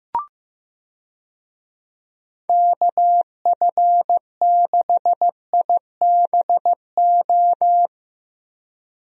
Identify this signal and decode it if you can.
Morse code, heard as a keyed tone, about 15 wpm, 700 Hz. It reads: KF6IBO
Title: USA Call Signs - 6 Characters - Repeated 2x - 15wpm